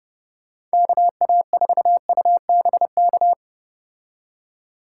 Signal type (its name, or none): Morse code